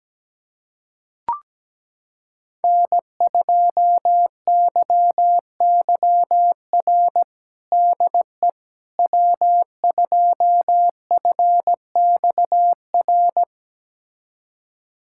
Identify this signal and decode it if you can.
Morse code, heard as a keyed tone, about 17 wpm, 700 Hz. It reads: N2YYR DE W2FXR
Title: USA Callsign DE Callsign - 5 Characters - 17wpm